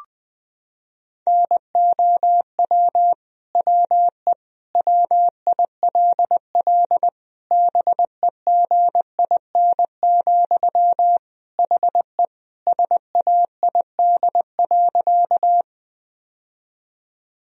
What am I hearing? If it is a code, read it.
Morse code, heard as a keyed tone, about 21 wpm, 700 Hz. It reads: NOW WE WILL BEGIN, HE SAID.
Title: Sentences from Top 1000 Words - Review Part 1 - 20wpm